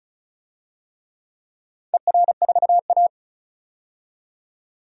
Morse code, heard as a keyed tone, about 34 wpm, 700 Hz. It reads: ER4A
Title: DXCC-100 Call Signs - 4 Characters - 35wpm